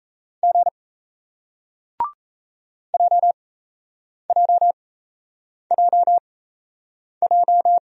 Morse code, heard as a keyed tone, about 38 wpm, spaced as for 8 wpm, 700 Hz. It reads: G JJJJ